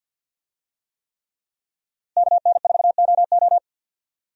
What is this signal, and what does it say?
Morse code, heard as a keyed tone, about 51 wpm, 700 Hz. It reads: KN4OO
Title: USA Call Signs - 5 Characters - Repeated 3x - 50wpm